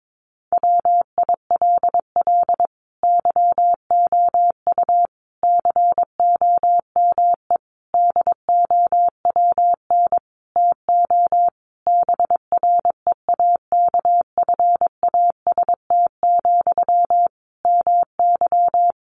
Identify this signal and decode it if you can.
Morse code, heard as a keyed tone, about 22 wpm, 700 Hz. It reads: WILL YOU COME DOWN TO BREAKFAST, MY
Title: Sentences from Top 1500 Words 22wpm